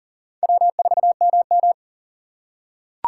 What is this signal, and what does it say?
Morse code, heard as a keyed tone, about 37 wpm, 700 Hz. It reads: W4MM
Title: USA Call Signs - 2x Character Spacing - 40wpm